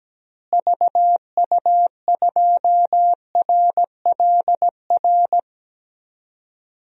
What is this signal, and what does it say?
Morse code, heard as a keyed tone, about 17 wpm, 700 Hz. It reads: VU2RLR
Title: DXCC-100 Call Signs - 6 Characters - Repeated 3x - 17wpm